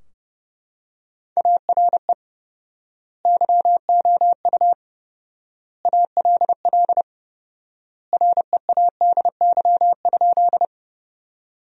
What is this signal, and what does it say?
Morse code, heard as a keyed tone, about 30 wpm, 700 Hz. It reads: ARE YOU ALL READY?
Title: 4x Word-Spacing - Sentences from Top 400 Words - 30wpm